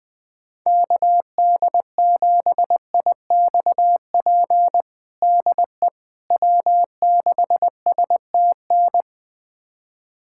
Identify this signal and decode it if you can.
Morse code, heard as a keyed tone, about 20 wpm, 700 Hz. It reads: KD7IXP DE W6STN
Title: USA Callsign DE Callsign 20wpm